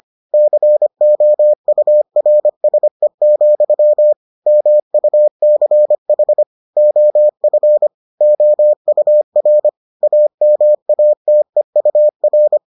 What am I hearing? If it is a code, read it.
Morse code, heard as a keyed tone, about 25 wpm, 600 Hz. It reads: COURSE, MUCH OF OUR AMATEUR